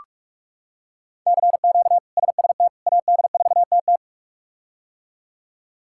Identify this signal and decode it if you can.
Morse code, heard as a keyed tone, about 46 wpm, 700 Hz. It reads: CQ SST AD4TT
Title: K1USN SST QSO Practice - 45wpm